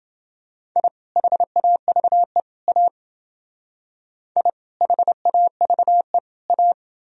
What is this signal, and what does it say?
Morse code, heard as a keyed tone, about 28 wpm, 700 Hz. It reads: I HAVE A I HAVE A